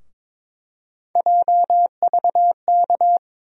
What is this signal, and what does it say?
Morse code, heard as a keyed tone, about 22 wpm, 700 Hz. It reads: JVK
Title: USA Call Sign Postfixes - 3 Letters 22wpm